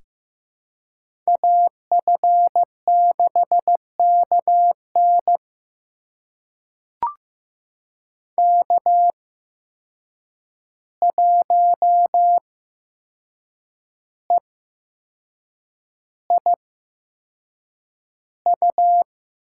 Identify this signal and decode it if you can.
Morse code, heard as a keyed tone, about 15 wpm, spaced as for 6 wpm, 700 Hz. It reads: AF6KN K1EIU
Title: USA Call Signs - 5 Characters - 8x Character Spacing - 15wpm